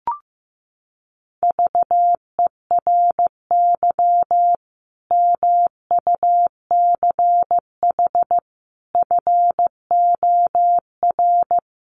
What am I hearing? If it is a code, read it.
Morse code, heard as a keyed tone, about 15 wpm, 700 Hz. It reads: VERY MUCH FOR
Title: Sets of 5 Words 15wpm